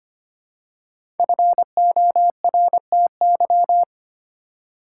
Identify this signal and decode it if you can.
Morse code, heard as a keyed tone, about 25 wpm, 700 Hz. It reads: FORTY
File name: 5 Letter Words - First 3 Letters Given - 25wpm